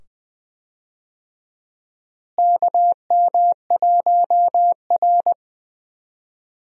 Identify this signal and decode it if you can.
Morse code, heard as a keyed tone, about 20 wpm, 700 Hz. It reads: KM1R
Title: USA Call Signs - 4 Characters - Repeated 2x - 20wpm